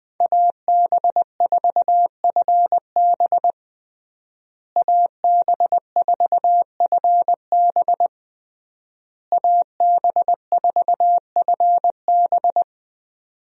Morse code, heard as a keyed tone, about 20 wpm, 700 Hz. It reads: AB4FB AB4FB AB4FB
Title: USA Call Signs - 5 Characters - Repeated 3x - 20wpm